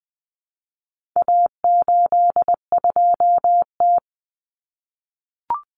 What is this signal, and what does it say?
Morse code, heard as a keyed tone, about 20 wpm, 700 Hz. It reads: A82T
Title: Common Cut Numbers - 4-Digit 20wpm